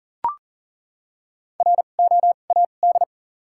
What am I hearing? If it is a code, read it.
Morse code, heard as a keyed tone, about 40 wpm, spaced as for 29 wpm, 700 Hz. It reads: ROAD
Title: 2x Character Spacing - Top 500 Words - 40wpm